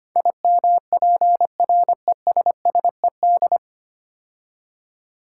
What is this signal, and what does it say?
Morse code, heard as a keyed tone, about 25 wpm, 700 Hz. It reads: IMPRESSED
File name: practice / 25wpm / 9 Letter Words - First 3 Letters Given - 25wpm